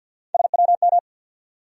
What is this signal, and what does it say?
Morse code, heard as a keyed tone, about 51 wpm, 700 Hz. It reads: SWM
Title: USA Call Sign Postfixes - 3 Letters 50wpm